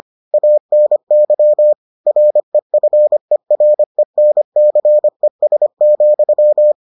Morse code, heard as a keyed tone, about 25 wpm, 600 Hz. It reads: ANY REFERENCES,